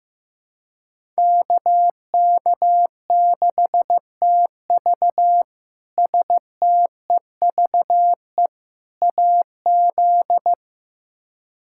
Morse code, heard as a keyed tone, about 16 wpm, 700 Hz. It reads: KK6TV STEVE AZ
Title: K1USN SST QSO Practice - 15wpm